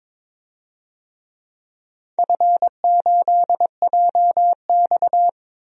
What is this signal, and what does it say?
Morse code, heard as a keyed tone, about 21 wpm, 700 Hz. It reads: F8JX
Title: DXCC-100 Call Signs - 4 Characters - Repeated 3x - 22wpm